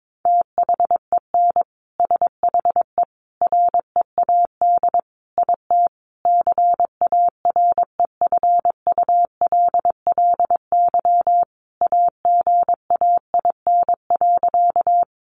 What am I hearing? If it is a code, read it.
Morse code, heard as a keyed tone, about 22 wpm, 700 Hz. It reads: THEN SHE READ IT CAREFULLY AGAIN.
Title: Sentences from Top 1300 Words 22wpm